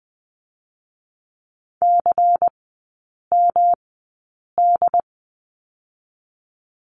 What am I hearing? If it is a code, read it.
Morse code, heard as a keyed tone, about 20 wpm, 700 Hz. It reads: C M D
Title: 107 Lesson 18e - Letter m - Three Characters 20wpm